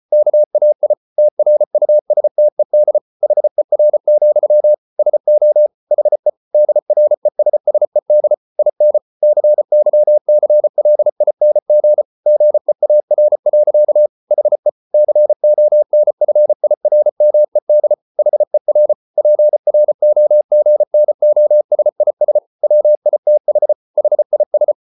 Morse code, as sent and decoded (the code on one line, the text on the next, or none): -.- .- .. / - .-. ..- ... - . -.. / .... .-. --..-- / ... --- / .... / -.. .-. ... ... -.. / .. -. / -.-. -.-- -.-. .-.. .. -. --. / --. .- .-. .-.-.- / .... / -.-. --- -. ..-. .. .-. -- . -.. / .... .-. / .--. .-. --- --. -. --- ... .. ... / .-- .. - .... / .... .. ...
KAI TRUSTED HER, SO HE DRESSED IN CYCLING GEAR. HE CONFIRMED HER PROGNOSIS WITH HIS